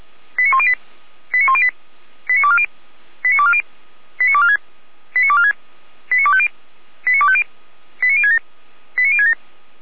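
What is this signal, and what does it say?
Five-tone selective call (ZVEI), heard as quick runs of five tones, tones of about 70 ms. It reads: 89108 89108 89240 89240 89246 89246 89250 89250 89067 89067
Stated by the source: ZVEI five-tone call 89337, received by SDR, from feuerwehr-orlamuende (fire brigade Orlamünde).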